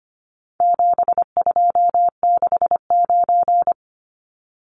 Morse code, heard as a keyed tone, about 25 wpm, 700 Hz. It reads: 7269
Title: Numbers - 4-Digit 25wpm